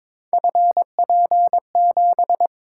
Morse code, heard as a keyed tone, about 22 wpm, 700 Hz. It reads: FP7
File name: practice / 22wpm / Three Letter-Number Combinations - 22wpm